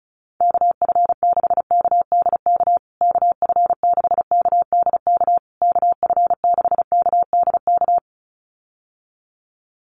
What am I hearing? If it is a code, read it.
Morse code, heard as a keyed tone, about 34 wpm, 700 Hz. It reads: KF6KDK KF6KDK KF6KDK
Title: USA Call Signs Repeated 3x 35wpm